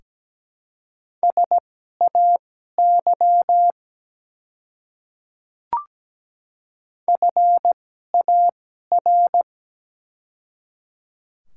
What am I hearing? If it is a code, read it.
Morse code, heard as a keyed tone, about 17 wpm, spaced as for 12 wpm, 700 Hz. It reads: SAY FAR
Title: 2x Character Spacing - Top 300 Words - 17wpm